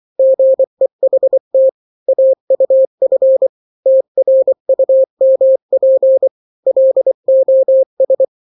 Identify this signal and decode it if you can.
Morse code, heard as a keyed tone, about 24 wpm, 530 Hz. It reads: GEHT AUF TRUMP LOS
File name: daily / 23wpm